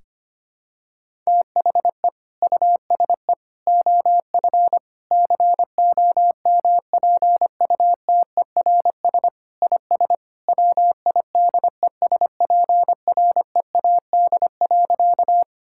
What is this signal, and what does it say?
Morse code, heard as a keyed tone, about 25 wpm, 700 Hz. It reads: THE USE OF COMPUTERS IS WIDESPREAD.
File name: practice / 25wpm / Sentences with 3-Word Phrases - 25wpm